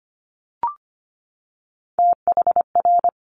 Morse code, heard as a keyed tone, about 25 wpm, 700 Hz. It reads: THR
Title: Multi-letter Phonemes 25wpm